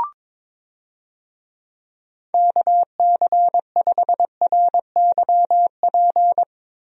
Morse code, heard as a keyed tone, about 22 wpm, 700 Hz. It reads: KC5RYP
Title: USA Call Signs - 6 Characters - Repeated 2x - 22wpm